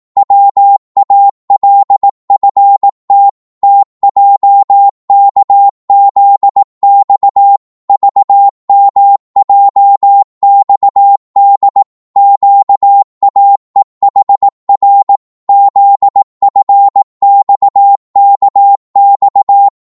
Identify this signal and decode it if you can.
Morse code, heard as a keyed tone, about 18 wpm, 800 Hz. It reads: WALFT TJKZX VMJXD QAEHR ZFXKX
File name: Morse / ltr-18wpm